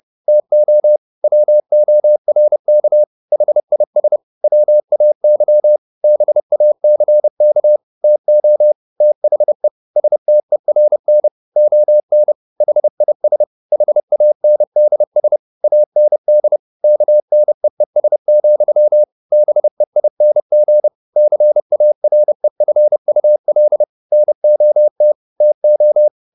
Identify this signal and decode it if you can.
Morse code, heard as a keyed tone, about 30 wpm, 600 Hz. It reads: TO WORK HIS WAY BACK TO THE STERN ON HIS HANDS AND KNEES, BEING CAREFUL NOT TO